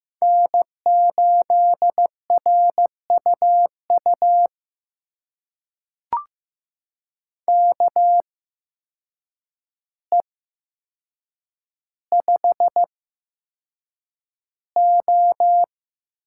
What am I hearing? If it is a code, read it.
Morse code, heard as a keyed tone, about 15 wpm, spaced as for 7 wpm, 700 Hz. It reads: N8RUU KE5O
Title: USA Call Signs - 8x Character Spacing - 15wpm